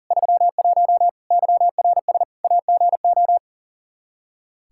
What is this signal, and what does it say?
Morse code, heard as a keyed tone, about 40 wpm, 700 Hz. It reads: 31 YRS AGO
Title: Practice QSO - Exchange 3 - More Info - 40wpm